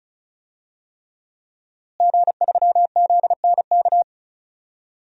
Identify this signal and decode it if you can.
Morse code, heard as a keyed tone, about 36 wpm, 700 Hz. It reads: G3ZNK
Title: DXCC-100 Call Signs - 5 Characters - Repeated 3x - 35wpm